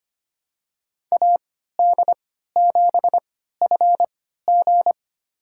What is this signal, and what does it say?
Morse code, heard as a keyed tone, about 25 wpm, spaced as for 14 wpm, 700 Hz. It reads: AD7FG
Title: USA Call Signs - 3x Character Spacing - 25wpm